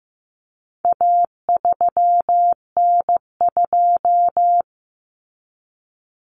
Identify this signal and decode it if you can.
Morse code, heard as a keyed tone, about 15 wpm, 700 Hz. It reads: A3N2
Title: Common Cut Numbers - 4-Digit 15wpm